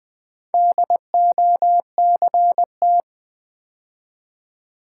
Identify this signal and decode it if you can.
Morse code, heard as a keyed tone, about 20 wpm, 700 Hz. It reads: DOCT
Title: Build A Word - 2 Letters At A Time - 8-Letter Words - 20wpm